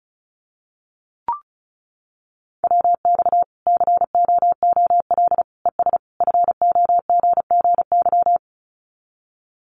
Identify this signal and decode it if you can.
Morse code, heard as a keyed tone, about 34 wpm, 700 Hz. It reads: WX COOL ES FOGGY
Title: Common QSO Phrases 35wpm